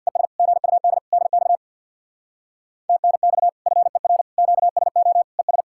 Morse code, heard as a keyed tone, about 64 wpm, 700 Hz. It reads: ES GUD DX TNX FER QSO ES